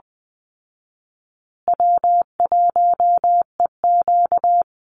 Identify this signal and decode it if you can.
Morse code, heard as a keyed tone, about 20 wpm, 700 Hz. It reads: W1EQ
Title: Easy Call Signs Repeated 2x - 20wpm